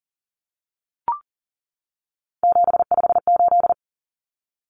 Morse code, heard as a keyed tone, about 40 wpm, 700 Hz. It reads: 758
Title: Common Cut Numbers - 3-Digit 40wpm